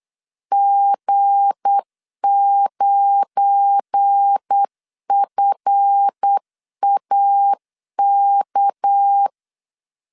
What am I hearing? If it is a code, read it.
Morse code, heard as a keyed tone, about 8 wpm, 800 Hz. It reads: G9FAK
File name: LEZIONI / 8 YJ45 / 8 YJ4 8wpm 3246